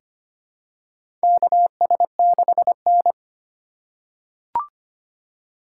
Morse code, heard as a keyed tone, about 25 wpm, 700 Hz. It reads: KS6N
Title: USA Call Signs - 4 Characters - 8x Character Spacing - 25wpm